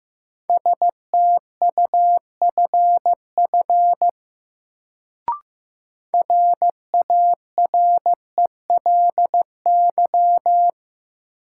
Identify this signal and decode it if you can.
Morse code, heard as a keyed tone, about 15 wpm, 700 Hz. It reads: STUFF RARELY
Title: Sentences from Top 1400 Words - New Word Drill - 15wpm